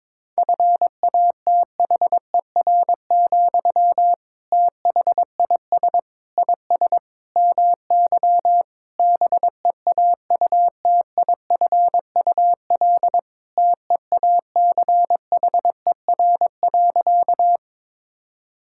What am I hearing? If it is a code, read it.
Morse code, heard as a keyed tone, about 22 wpm, 700 Hz. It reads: FATHER, THIS IS MY BEAUTIFUL TEACHER.
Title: Sentences from Top 1000 Words - Review Part 6 - 22wpm